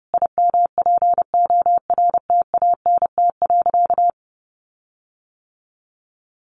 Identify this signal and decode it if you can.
Morse code, heard as a keyed tone, about 30 wpm, 700 Hz. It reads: IMPORTANT.
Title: Sentences from Top 1300 Words 30wpm